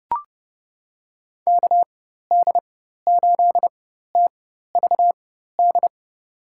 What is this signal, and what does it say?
Morse code, heard as a keyed tone, about 32 wpm, spaced as for 14 wpm, 700 Hz. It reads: KD8TVD
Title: USA Call Signs - 6 Characters - 4x Character Spacing - 30wpm